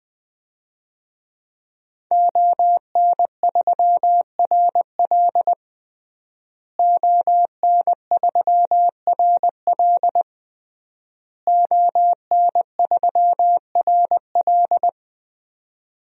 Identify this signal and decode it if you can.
Morse code, heard as a keyed tone, about 21 wpm, 700 Hz. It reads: ON3RL ON3RL ON3RL
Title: DXCC-100 Call Signs - 5 Characters - Repeated 3x - 20wpm